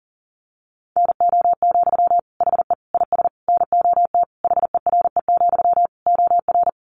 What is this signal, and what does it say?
Morse code, heard as a keyed tone, about 40 wpm, 700 Hz. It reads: NO, HE IS NOT HERE, OR
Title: Sentences from Top 1400 Words 40wpm